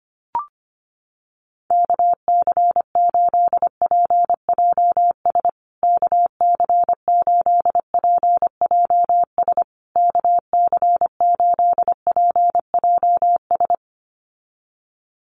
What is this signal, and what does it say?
Morse code, heard as a keyed tone, about 25 wpm, 700 Hz. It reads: KC8PJS KC8PJS KC8PJS
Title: USA Call Signs Repeated 3x 25wpm